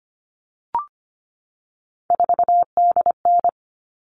Morse code, heard as a keyed tone, about 25 wpm, 700 Hz. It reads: 4DN
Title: All Cut Numbers - 3-Digit 25wpm